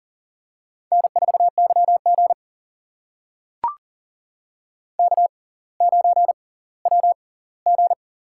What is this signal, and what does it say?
Morse code, heard as a keyed tone, about 40 wpm, spaced as for 21 wpm, 700 Hz. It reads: N4YG K9WG
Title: USA Call Signs - 4 Characters - 6x Character Spacing - 40wpm